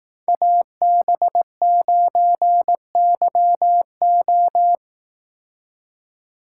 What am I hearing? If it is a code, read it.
Morse code, heard as a keyed tone, about 18 wpm, 700 Hz. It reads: AB9YO
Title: Speed Racing - USA Call Signs 15wpm